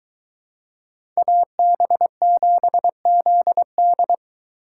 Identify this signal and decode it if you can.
Morse code, heard as a keyed tone, about 21 wpm, 700 Hz. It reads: AB7ZD